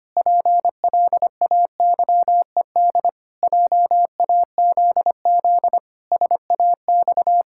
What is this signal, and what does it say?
Morse code, heard as a keyed tone, about 25 wpm, 700 Hz. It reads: PLAYED JAZZ SAX
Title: Pangram Challenge v2 - 25wpm